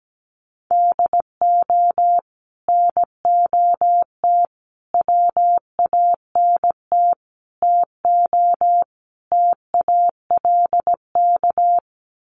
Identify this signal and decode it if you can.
Morse code, heard as a keyed tone, about 17 wpm, 700 Hz. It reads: DO NOT WANT TO TALK
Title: Sets of 5 Words 17wpm